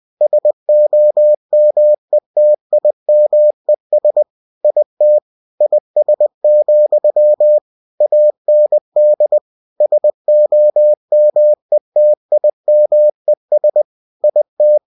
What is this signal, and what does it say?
Morse code, heard as a keyed tone, about 20 wpm, 600 Hz. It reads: SOMETIMES IT IS, AND SOMETIMES IT
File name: winnie-20wpm